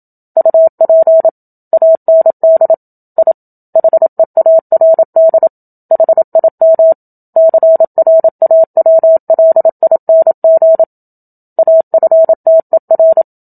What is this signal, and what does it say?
Morse code, heard as a keyed tone, about 27 wpm, 650 Hz. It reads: UP AND I HEARD HIM CRAWLING AFTER
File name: Book Two / 01-Chapter 1 - 27 WPM